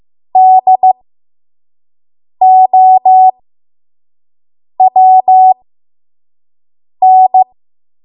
Morse code, heard as a keyed tone, about 15 wpm, spaced as for 5 wpm, 750 Hz. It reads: DOWN
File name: Archive / 5 WPM / 05WPM